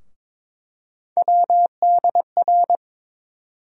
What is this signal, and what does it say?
Morse code, heard as a keyed tone, about 21 wpm, 700 Hz. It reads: WDR
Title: USA Call Sign Postfixes - 3 Letters 22wpm